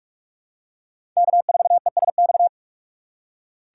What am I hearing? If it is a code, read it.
Morse code, heard as a keyed tone, about 46 wpm, 700 Hz. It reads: K4ESX
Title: USA Call Signs - 3x Character Spacing - 45wpm